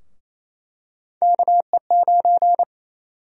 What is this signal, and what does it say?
Morse code, heard as a keyed tone, about 28 wpm, 700 Hz. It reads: KE9
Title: USA Call Sign Prefixes - 28wpm